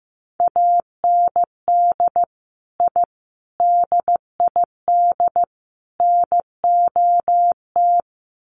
Morse code, heard as a keyed tone, about 15 wpm, 700 Hz. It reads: AND I DID NOT
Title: Sets of 4 Words 15wpm